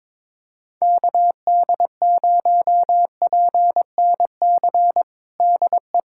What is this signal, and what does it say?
Morse code, heard as a keyed tone, about 22 wpm, 700 Hz. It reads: KD0PNC DE
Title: USA Callsign DE Callsign - 6 Characters - 22wpm